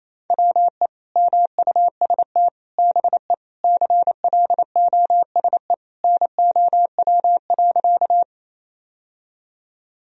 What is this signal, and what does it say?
Morse code, heard as a keyed tone, about 28 wpm, 700 Hz. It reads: WE MUST BE CLOSE NOW.
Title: Sentences from Top 1000 Words - Review Part 1 - 28wpm